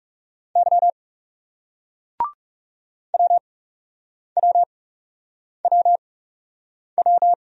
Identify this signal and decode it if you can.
Morse code, heard as a keyed tone, about 39 wpm, spaced as for 8 wpm, 700 Hz. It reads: Y WWWW